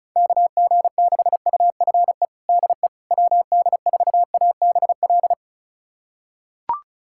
Morse code, heard as a keyed tone, about 36 wpm, 700 Hz. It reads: KG6UFE DE WD4ABL
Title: USA Callsign DE Callsign 35wpm